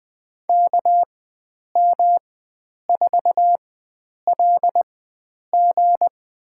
Morse code, heard as a keyed tone, about 20 wpm, spaced as for 9 wpm, 700 Hz. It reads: KM4LG
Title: USA Call Signs - 4x Character Spacing - 20wpm